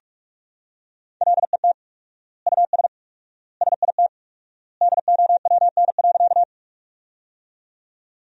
Morse code, heard as a keyed tone, about 46 wpm, 700 Hz. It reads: LET US SIT DOWN.
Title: 4x Word-Spacing - Sentences from Top 500 Words - 45wpm